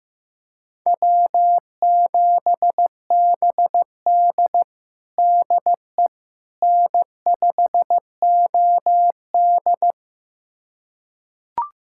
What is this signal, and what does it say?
Morse code, heard as a keyed tone, about 15 wpm, 700 Hz. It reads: W7BD DE N5OD
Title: USA Callsign DE Callsign - 4 Characters - 15wpm